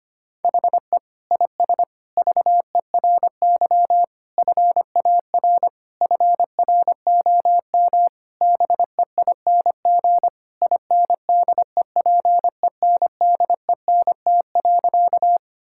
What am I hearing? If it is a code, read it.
Morse code, heard as a keyed tone, about 25 wpm, 700 Hz. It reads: HE IS VERY FAR FROM BEING INDEPENDENT.
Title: Sentences from Top 1000 Words - Review Part 6 - 25wpm